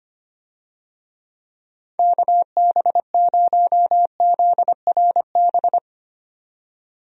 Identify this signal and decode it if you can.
Morse code, heard as a keyed tone, about 25 wpm, 700 Hz. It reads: KB0ZRB
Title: USA Call Signs - 6 Characters - 25wpm